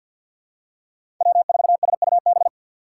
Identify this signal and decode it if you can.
Morse code, heard as a keyed tone, about 52 wpm, 700 Hz. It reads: W4SUB
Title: USA Call Signs - 5 Characters - 3x Character Spacing - 50wpm